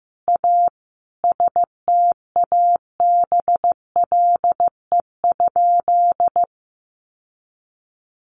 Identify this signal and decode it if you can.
Morse code, heard as a keyed tone, about 15 wpm, 700 Hz. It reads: A STABLE?
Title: Sentences from Top 1500 Words 15wpm